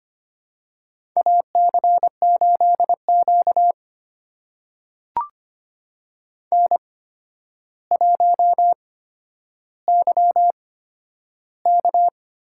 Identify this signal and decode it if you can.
Morse code, heard as a keyed tone, about 25 wpm, spaced as for 11 wpm, 700 Hz. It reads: AC8Q N1YK
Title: USA Call Signs - 8x Character Spacing - 25wpm